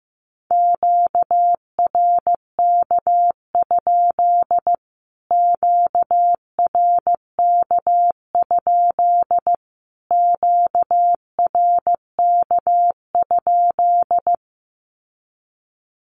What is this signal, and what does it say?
Morse code, heard as a keyed tone, about 15 wpm, 700 Hz. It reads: QRK? QRK? QRK?
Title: Q-Codes Repeated 3x 15wpm